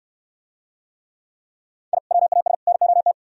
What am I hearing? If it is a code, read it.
Morse code, heard as a keyed tone, about 67 wpm, 700 Hz. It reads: I WAS NOT